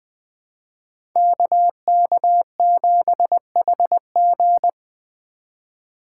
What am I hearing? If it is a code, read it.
Morse code, heard as a keyed tone, about 20 wpm, 700 Hz. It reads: KK7HG
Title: Speed Racing - USA 2x2 Call Signs 17wpm